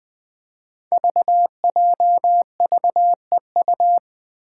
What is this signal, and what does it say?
Morse code, heard as a keyed tone, about 20 wpm, 700 Hz. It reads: VJVEU
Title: Five Challenging Letter-Number Combinations 20wpm